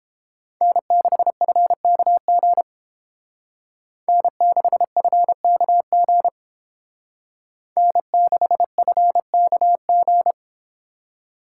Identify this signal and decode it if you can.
Morse code, heard as a keyed tone, about 29 wpm, 700 Hz. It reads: N6FKG N6FKG N6FKG